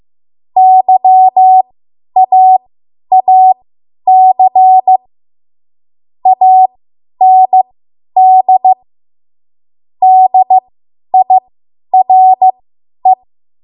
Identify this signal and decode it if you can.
Morse code, heard as a keyed tone, about 15 wpm, spaced as for 10 wpm, 750 Hz. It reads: YAAC AND DIRE
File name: Morse / 10WPM